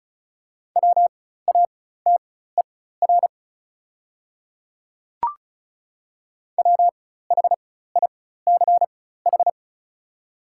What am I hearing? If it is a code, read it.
Morse code, heard as a keyed tone, about 36 wpm, spaced as for 16 wpm, 690 Hz. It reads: WATER WHICH
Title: 4x Character Spacing - Top 100 Words - 35wpm